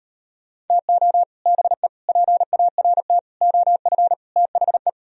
Morse code, heard as a keyed tone, about 38 wpm, 700 Hz. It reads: TO BE PART OF THE